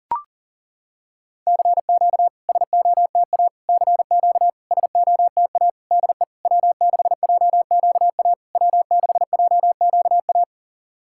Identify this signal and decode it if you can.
Morse code, heard as a keyed tone, about 40 wpm, 700 Hz. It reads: CQ SOTA CQ SOTA DE W6JQA W6JQA
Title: SOTA QSO Practice 40wpm